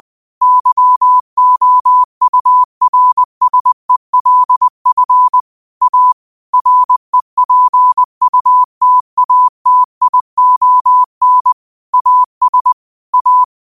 Morse code, heard as a keyed tone, about 20 wpm, 1000 Hz. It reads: YOURSELF A REPUTATION AS A